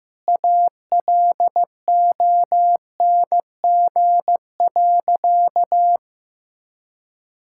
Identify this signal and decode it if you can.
Morse code, heard as a keyed tone, about 15 wpm, 700 Hz. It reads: ALONG.
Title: Sentences from Top 1600 Words 15wpm